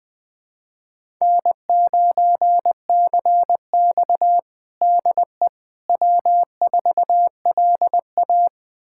Morse code, heard as a keyed tone, about 21 wpm, 700 Hz. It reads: N9CX DE W4LA
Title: USA Callsign DE Callsign - 4 Characters - 20wpm